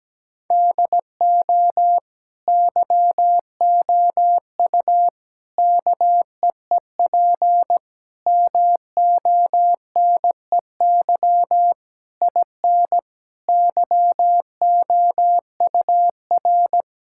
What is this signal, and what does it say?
Morse code, heard as a keyed tone, about 17 wpm, 700 Hz. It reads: DO YOU KEEP MONEY IN YOUR
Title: Sentences from Top 1000 Words - Review Part 6 - 17wpm